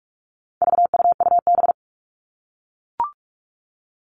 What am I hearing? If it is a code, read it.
Morse code, heard as a keyed tone, about 44 wpm, 700 Hz. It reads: VUUB
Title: All Cut Numbers - 4-Digit 45wpm